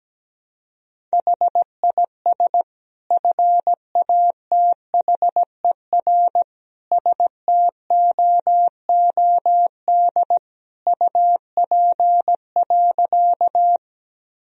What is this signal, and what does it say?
Morse code, heard as a keyed tone, about 17 wpm, 700 Hz. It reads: HIS FATHER STOOD UP.
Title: Sentences from Top 1000 Words - Review Part 2 - 17wpm